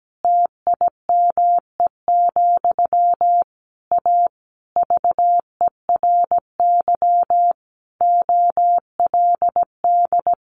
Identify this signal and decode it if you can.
Morse code, heard as a keyed tone, about 17 wpm, 700 Hz. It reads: TIME, A VERY OLD
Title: A Sound Of Thunder 17wpm